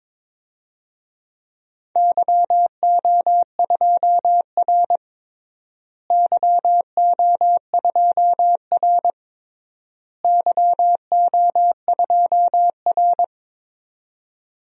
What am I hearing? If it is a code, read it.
Morse code, heard as a keyed tone, about 21 wpm, 700 Hz. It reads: YO2R YO2R YO2R
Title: DXCC-100 Call Signs - 4 Characters - Repeated 3x - 22wpm